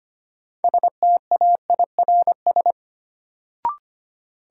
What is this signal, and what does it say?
Morse code, heard as a keyed tone, about 25 wpm, 700 Hz. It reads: STAIRS
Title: Sets of 5 Words Encore - New Word Drill - 25wpm